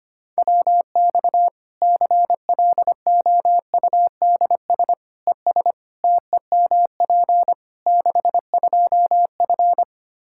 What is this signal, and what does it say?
Morse code, heard as a keyed tone, about 25 wpm, 700 Hz. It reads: WX CLOUDS ES TEMP 62F
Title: K1USN SST QSO Practice 25wpm